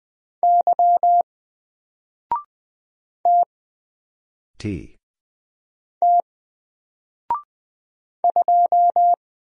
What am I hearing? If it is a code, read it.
Morse code, heard as a keyed tone, about 20 wpm, spaced as for 3 wpm, 700 Hz. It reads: Y TT 2
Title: Single Letter-Number 20wpm